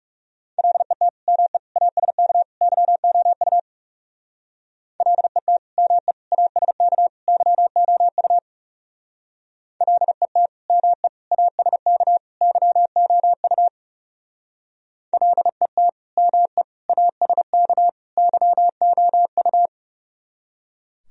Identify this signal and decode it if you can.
Morse code, heard as a keyed tone, about 37 wpm, 700 Hz. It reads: LET ME ASK YOU LET ME ASK YOU LET ME ASK YOU LET ME ASK YOU
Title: Speed-Racing - Sets of 4 Words 30wpm